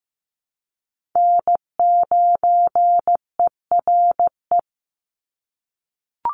Morse code, heard as a keyed tone, about 15 wpm, 700 Hz. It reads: N9ERE
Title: USA Call Signs 15wpm